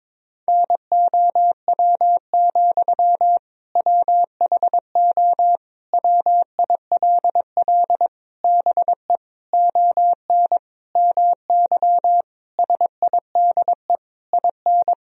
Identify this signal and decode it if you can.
Morse code, heard as a keyed tone, about 22 wpm, 700 Hz. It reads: NOW, WHO WILL BE ON MY SIDE IN
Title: Sentences from Top 1000 Words - Review Part 1 - 22wpm